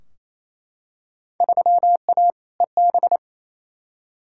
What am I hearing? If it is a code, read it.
Morse code, heard as a keyed tone, about 28 wpm, 700 Hz. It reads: 3A EB
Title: Field Day Exchanges Encore - 28wpm